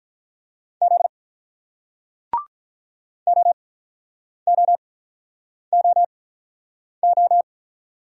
Morse code, heard as a keyed tone, about 48 wpm, spaced as for 9 wpm, 700 Hz. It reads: Z OOOO